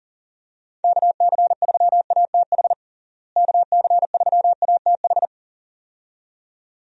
Morse code, heard as a keyed tone, about 40 wpm, 700 Hz. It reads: KC3ATH KC3ATH
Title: USA Call Signs - 6 Characters - Repeated 2x - 40wpm